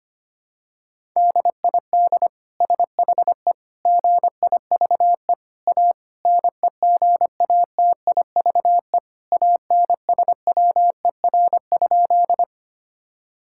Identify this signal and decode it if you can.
Morse code, heard as a keyed tone, about 25 wpm, 700 Hz. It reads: DID SHE GIVE A NEGATIVE ANSWER?
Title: Sentences from Top 1000 Words - Review Part 6 - 25wpm